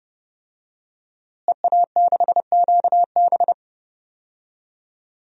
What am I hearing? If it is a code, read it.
Morse code, heard as a keyed tone, about 31 wpm, 700 Hz. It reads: EA6QB
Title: DXCC-100 Call Signs - 5 Characters - Repeated 2x - 30wpm